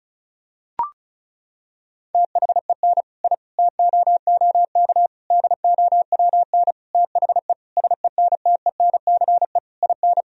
Morse code, heard as a keyed tone, about 34 wpm, 700 Hz. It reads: THEN I TOOK DOWN THE SENTENCE IN
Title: Sentences from Top 1000 Words - Review Part 1 - 35wpm